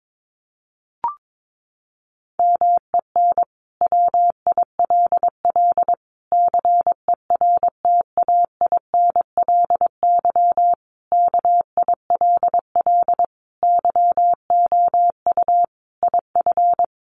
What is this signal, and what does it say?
Morse code, heard as a keyed tone, about 22 wpm, 700 Hz. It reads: MEN WILL CERTAINLY KILL YOU IF
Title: Sentences from Top 1500 Words 22wpm